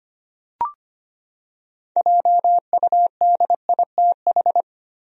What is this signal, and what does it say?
Morse code, heard as a keyed tone, about 25 wpm, 700 Hz. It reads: JUDITH
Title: USA Common Names - 25wpm